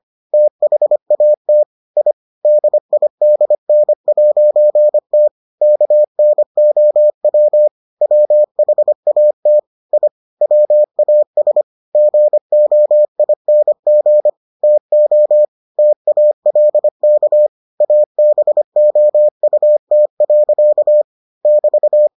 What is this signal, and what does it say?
Morse code, heard as a keyed tone, about 25 wpm, 600 Hz. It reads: THAT I DIDN'T KNOW WHAT I WAS GOING TO TALK ABOUT. <BT>